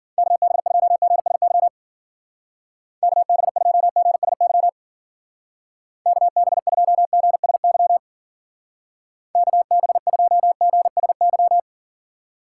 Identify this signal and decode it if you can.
Morse code, heard as a keyed tone, about 51 wpm, 700 Hz. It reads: KB2GSY KB2GSY KB2GSY KB2GSY